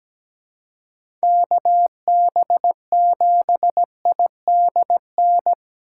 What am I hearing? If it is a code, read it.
Morse code, heard as a keyed tone, about 17 wpm, 700 Hz. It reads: KB7IDN
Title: USA Call Signs - 6 Characters - 4x Character Spacing - 17wpm